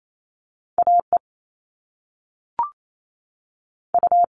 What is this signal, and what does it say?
Morse code, heard as a keyed tone, about 28 wpm, 700 Hz. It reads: A E U
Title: All Cut Numbers - 2-Digit 28wpm